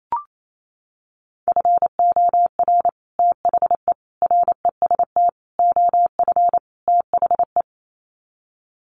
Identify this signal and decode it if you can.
Morse code, heard as a keyed tone, about 28 wpm, 700 Hz. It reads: FOR THE REST OF THE
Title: Sets of 5 Words 28wpm